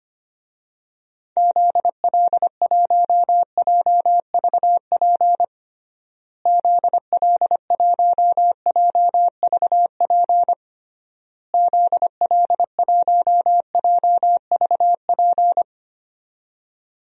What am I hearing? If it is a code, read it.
Morse code, heard as a keyed tone, about 25 wpm, 700 Hz. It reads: ZL1JVP ZL1JVP ZL1JVP
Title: DXCC-100 Call Signs - 6 Characters - Repeated 3x - 25wpm